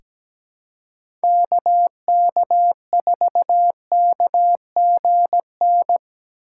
Morse code, heard as a keyed tone, about 17 wpm, 700 Hz. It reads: KK4KGN
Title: USA Call Signs - 6 Characters - 4x Character Spacing - 17wpm